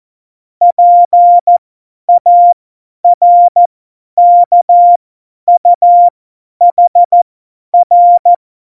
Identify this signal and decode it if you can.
Morse code, heard as a keyed tone, about 14 wpm, spaced as for 10 wpm, 700 Hz. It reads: PARKUHR
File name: Morsekurs / Witze / Witz 11. September 2018, 50 F70